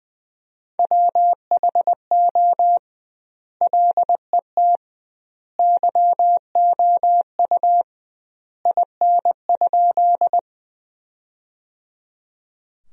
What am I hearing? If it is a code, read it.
Morse code, heard as a keyed tone, about 20 wpm, 700 Hz. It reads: WHO LET YOU IN?
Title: 2x Word-Spacing - Sentences from Top 300 Words - 20wpm